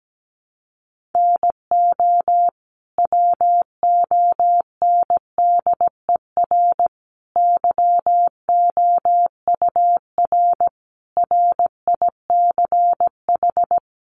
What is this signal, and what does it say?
Morse code, heard as a keyed tone, about 17 wpm, 700 Hz. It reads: NO WONDER YOUR RICH
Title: Sentences from Top 1100 Words 17wpm